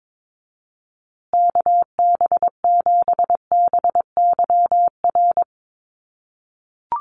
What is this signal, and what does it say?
Morse code, heard as a keyed tone, about 22 wpm, 700 Hz. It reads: KB7BYR
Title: USA Call Signs Repeated 2x 22wpm